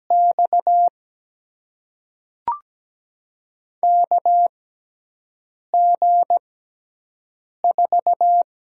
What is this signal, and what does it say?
Morse code, heard as a keyed tone, about 17 wpm, spaced as for 6 wpm, 700 Hz. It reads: X KG4
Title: USA Call Signs - 6 Characters - 6x Character Spacing - 17wpm